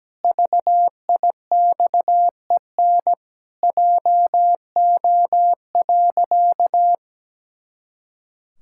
Morse code, heard as a keyed tone, about 17 wpm, 700 Hz. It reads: VIXEN JO.
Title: Pangram Challenge v2 - 17wpm